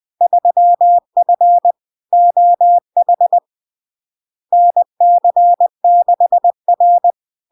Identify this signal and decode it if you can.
Morse code, heard as a keyed tone, about 20 wpm, 700 Hz. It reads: 3F OH NC6R
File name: LARK-FD-2022-Log-in-CW-20wpm-700-Hz